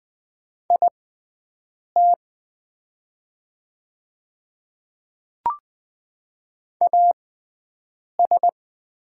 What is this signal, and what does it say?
Morse code, heard as a keyed tone, about 20 wpm, spaced as for 7 wpm, 700 Hz. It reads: IT AS
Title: 6x Character Spacing - Top 500 Words - 20wpm